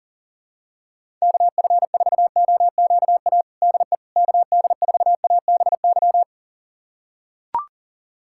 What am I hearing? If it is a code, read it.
Morse code, heard as a keyed tone, about 40 wpm, 700 Hz. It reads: KF4OQA DE KD4ABY
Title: USA Callsign DE Callsign 40wpm